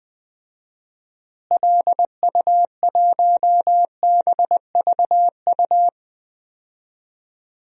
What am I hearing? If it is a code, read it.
Morse code, heard as a keyed tone, about 21 wpm, 700 Hz. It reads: LU1BVU
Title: DXCC-100 Call Signs - 6 Characters - Repeated 3x - 20wpm